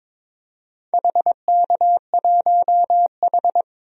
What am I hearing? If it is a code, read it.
Morse code, heard as a keyed tone, about 21 wpm, 700 Hz. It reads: HK1H
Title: DXCC-100 Call Signs - 4 Characters - 22wpm